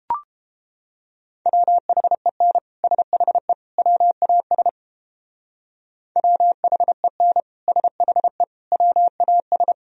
Morse code, heard as a keyed tone, about 29 wpm, 700 Hz. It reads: WHEN SHE WAS WHEN SHE WAS